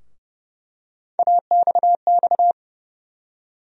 Morse code, heard as a keyed tone, about 31 wpm, 700 Hz. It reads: AXX
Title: USA Call Sign Postfixes - 3 Letters 30wpm